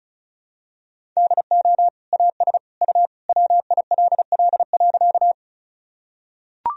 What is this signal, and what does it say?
Morse code, heard as a keyed tone, about 34 wpm, 700 Hz. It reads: DO AS U WILL.
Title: Sentences from Top 100 Words and Abbreviations 35wpm